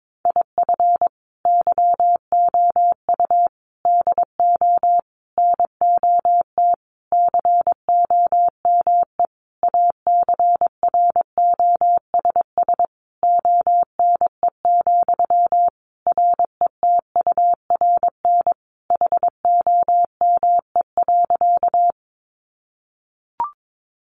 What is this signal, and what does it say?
Morse code, heard as a keyed tone, about 22 wpm, 700 Hz. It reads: IF YOU DO NOT COME ACROSS ONE, RETURN HOME.
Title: Sentences from Top 800 Words 22wpm